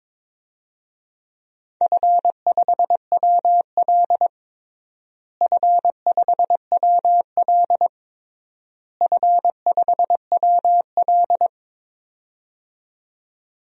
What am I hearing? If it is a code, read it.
Morse code, heard as a keyed tone, about 22 wpm, 700 Hz. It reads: F5WL F5WL F5WL
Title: DXCC-100 Call Signs - 4 Characters - Repeated 3x - 22wpm